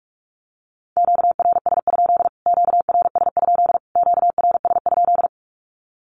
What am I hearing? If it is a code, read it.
Morse code, heard as a keyed tone, about 44 wpm, 700 Hz. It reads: QRS? QRS? QRS?
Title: Q-Codes Repeated 3x 45wpm